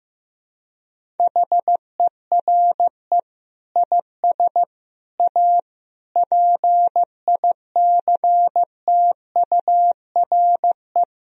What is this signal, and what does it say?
Morse code, heard as a keyed tone, about 15 wpm, 700 Hz. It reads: HERE IS A PICTURE
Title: Sentences from Top 1000 Words - Review Part 1 - 15wpm